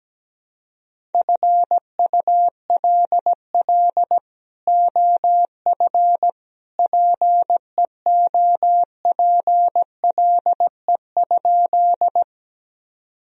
Morse code, heard as a keyed tone, about 17 wpm, 700 Hz. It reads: FULL OF PEOPLE?
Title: Sentences from Top 1000 Words - Review Part 3 - 17wpm